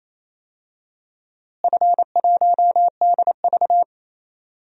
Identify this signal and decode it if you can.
Morse code, heard as a keyed tone, about 28 wpm, 700 Hz. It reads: F1DV
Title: DXCC-100 Call Signs - 4 Characters - Repeated 2x - 28wpm